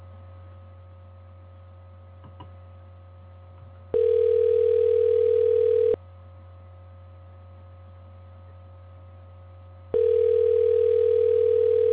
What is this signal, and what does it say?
Ringback tone (North America), 440 + 480 Hz.